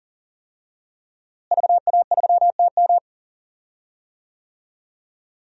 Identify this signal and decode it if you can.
Morse code, heard as a keyed tone, about 40 wpm, 700 Hz. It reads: VA3TM
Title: DXCC-100 Call Signs - 5 Characters - Repeated 2x - 40wpm